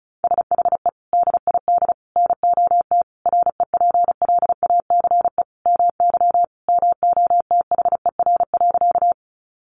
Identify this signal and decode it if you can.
Morse code, heard as a keyed tone, about 34 wpm, 700 Hz. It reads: SHE DID NOT REPLACE MY MOTHER.
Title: Sentences from Top 1400 Words 35wpm